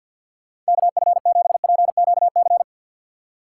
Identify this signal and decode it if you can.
Morse code, heard as a keyed tone, about 62 wpm, 700 Hz. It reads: KF7PQC
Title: Speed Racing - USA 2x3 Call Signs 50wpm